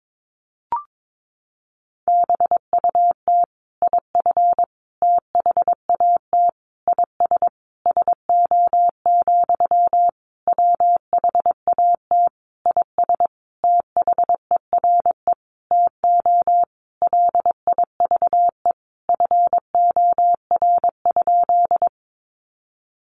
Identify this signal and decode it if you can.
Morse code, heard as a keyed tone, about 22 wpm, 700 Hz. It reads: BUT IF THAT IS SO, WHAT IS THERE TO LIVE FOR?
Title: Sentences from Top 200 Words 22wpm